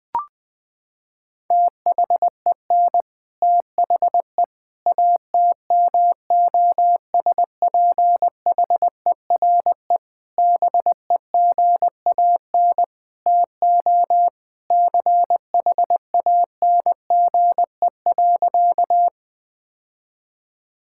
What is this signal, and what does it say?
Morse code, heard as a keyed tone, about 20 wpm, 700 Hz. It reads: THEN THE ATMOSPHERE BEGAN TO CHANGE.
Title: Sentences from Top 1600 Words 20wpm